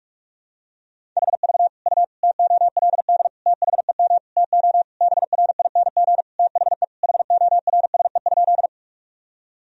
Morse code, heard as a keyed tone, about 46 wpm, 700 Hz. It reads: HV U TOLD THEM TO BRING THE HORSE?
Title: Sentences from Top 500 Words and Abbreviations 45wpm